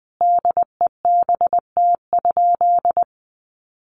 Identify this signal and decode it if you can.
Morse code, heard as a keyed tone, about 20 wpm, 700 Hz. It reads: DEBT?
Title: Sentences from Top 1300 Words 20wpm